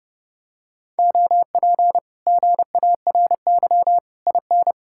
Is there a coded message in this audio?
Yes: Morse code from 1.0 to 4.7 s, heard as a keyed tone, about 30 wpm, 700 Hz. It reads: OP GARY IN